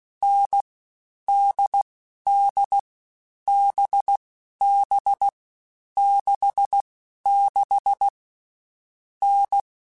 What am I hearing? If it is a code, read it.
Morse code, heard as a keyed tone, about 16 wpm, 800 Hz. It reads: N D D B B 6 6 N
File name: Difficult Combos / Difficult Combos-mp3s / NDB NDB6 @15